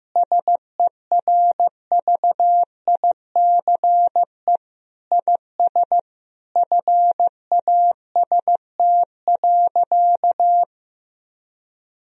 Morse code, heard as a keyed tone, about 15 wpm, 700 Hz. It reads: SERVICE IS FAST.
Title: Sentences from Top 1000 Words - Review Part 4 - 15wpm